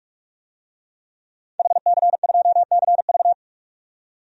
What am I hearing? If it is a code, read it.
Morse code, heard as a keyed tone, about 45 wpm, 700 Hz. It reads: HC2CV
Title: DXCC-100 Call Signs - 5 Characters - 45wpm